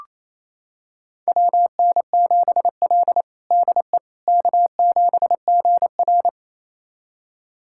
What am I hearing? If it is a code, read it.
Morse code, heard as a keyed tone, about 28 wpm, 700 Hz. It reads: WN7L DE K7GR
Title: USA Callsign DE Callsign - 4 Characters - 28wpm